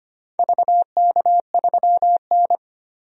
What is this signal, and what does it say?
Morse code, heard as a keyed tone, about 25 wpm, 700 Hz. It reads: VK3N
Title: DXCC-100 Call Signs - 4 Characters - 25wpm